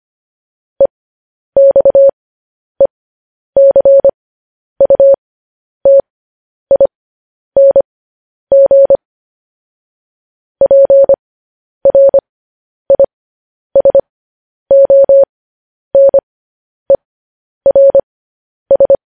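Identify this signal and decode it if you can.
Morse code, heard as a keyed tone, about 25 wpm, spaced as for 10 wpm, 550 Hz. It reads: EXECUTING PRISONERS
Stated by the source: Morse code news headlines.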